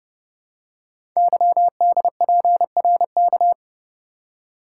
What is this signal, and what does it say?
Morse code, heard as a keyed tone, about 30 wpm, 690 Hz. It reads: YDPRK
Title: Code Groups - 5 Letters - 30wpm